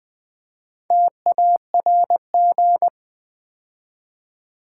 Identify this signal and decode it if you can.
Morse code, heard as a keyed tone, about 20 wpm, 700 Hz. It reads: TARG
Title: Build A Word - 2 Letters At A Time - 6-Letter Words - 20wpm